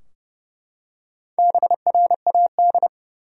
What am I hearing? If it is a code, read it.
Morse code, heard as a keyed tone, about 32 wpm, 700 Hz. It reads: BRAD
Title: USA Common Names - Max 4 Letters - 30wpm